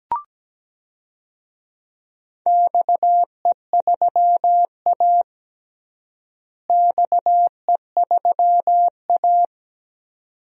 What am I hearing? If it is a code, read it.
Morse code, heard as a keyed tone, about 17 wpm, 700 Hz. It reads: XE3A XE3A
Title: DXCC-100 Call Signs - 4 Characters - Repeated 2x - 17wpm